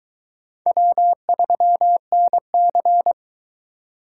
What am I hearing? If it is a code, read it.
Morse code, heard as a keyed tone, about 20 wpm, 700 Hz. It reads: W3NC